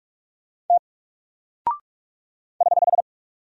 Morse code, heard as a keyed tone, about 45 wpm, 700 Hz. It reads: T <HH>
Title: Single Character (Letter:Number:Punctuation:Prosign) 45wpm